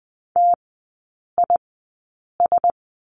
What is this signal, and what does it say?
Morse code, heard as a keyed tone, about 20 wpm, spaced as for 8 wpm, 700 Hz. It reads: TIS
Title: 041 Lesson 8e - Number 1 - Three Characters 20wpm